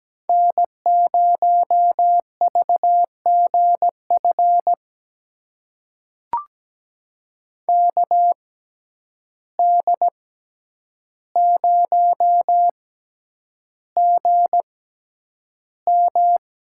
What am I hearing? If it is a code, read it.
Morse code, heard as a keyed tone, about 17 wpm, spaced as for 9 wpm, 700 Hz. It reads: N0VGF KD0GM
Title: USA Call Signs - 6x Character Spacing - 17wpm